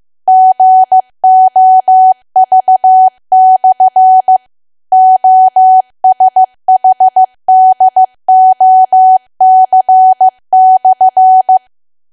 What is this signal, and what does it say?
Morse code, heard as a keyed tone, about 15 wpm, 750 Hz. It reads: GOV/OSHDOC/
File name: cwa / W1AW-Level2 / 15WPM